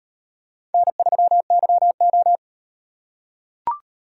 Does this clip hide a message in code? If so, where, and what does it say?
Yes: Morse code from 0.7 to 2.4 s, heard as a keyed tone, about 38 wpm, 700 Hz. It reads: N3YO